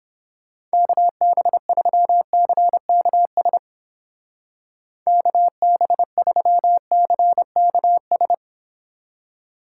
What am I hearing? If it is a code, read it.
Morse code, heard as a keyed tone, about 28 wpm, 700 Hz. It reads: KB3CKS KB3CKS